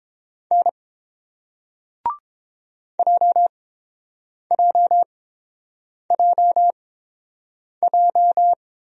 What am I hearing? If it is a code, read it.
Morse code, heard as a keyed tone, about 30 wpm, spaced as for 7 wpm, 700 Hz. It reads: N JJJJ